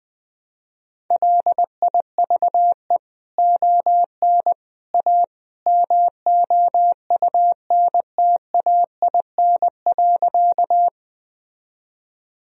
Morse code, heard as a keyed tone, about 20 wpm, 700 Hz. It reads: LIVE ON A MOUNTAIN.
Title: Sentences from Top 1000 Words - Review Part 1 - 20wpm